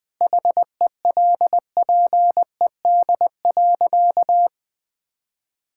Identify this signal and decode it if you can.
Morse code, heard as a keyed tone, about 20 wpm, 700 Hz. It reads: HELPED.
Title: Sentences with 4-Word Phrases - 20wpm